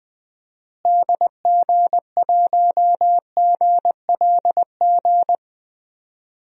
Morse code, heard as a keyed tone, about 20 wpm, 700 Hz. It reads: DG1GLG
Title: DXCC-100 Call Signs - 6 Characters - Repeated 2x - 20wpm